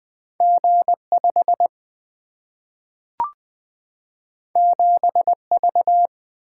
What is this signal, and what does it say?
Morse code, heard as a keyed tone, about 20 wpm, 700 Hz. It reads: G 5 7 V
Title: Single-Letter Single-Number 20wpm